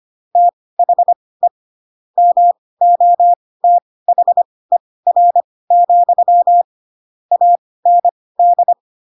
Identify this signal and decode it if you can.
Morse code, heard as a keyed tone, about 25 wpm, spaced as for 18 wpm, 700 Hz. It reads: THE MOTHER, AND
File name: Pinocchio ch3 12 25 ik1hge